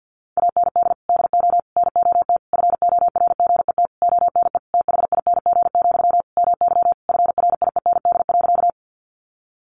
Morse code, heard as a keyed tone, about 48 wpm, 700 Hz. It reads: AND DO NOT FORGET ONE THING, MY FRIEND.
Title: Sentences from Top 1200 Words 50wpm